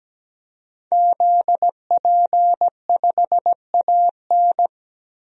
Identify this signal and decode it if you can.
Morse code, heard as a keyed tone, about 17 wpm, 700 Hz. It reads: ZP5AN
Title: DXCC-100 Call Signs - 5 Characters - Repeated 3x - 17wpm